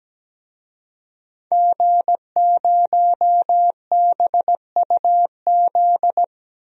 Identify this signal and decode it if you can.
Morse code, heard as a keyed tone, about 17 wpm, 700 Hz. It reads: G0BUZ
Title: DXCC-100 Call Signs - 5 Characters - 17wpm